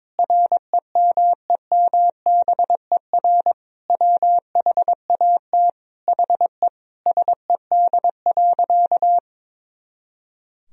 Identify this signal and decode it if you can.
Morse code, heard as a keyed tone, about 22 wpm, 700 Hz. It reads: REMEMBER WHAT HE SED.
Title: Sentences from Top 400 Words and Abbreviations 22wpm